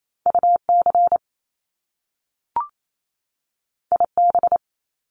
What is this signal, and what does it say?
Morse code, heard as a keyed tone, about 28 wpm, 700 Hz. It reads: UC IB
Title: Top Two Letter Combinations in Words 28wpm